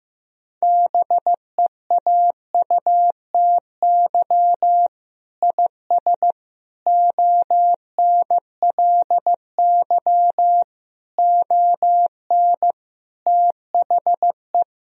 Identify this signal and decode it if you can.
Morse code, heard as a keyed tone, about 15 wpm, 700 Hz. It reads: BEAUTY IS ONLY ON THE
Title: Sentences from Top 1000 Words - Review Part 3 - 15wpm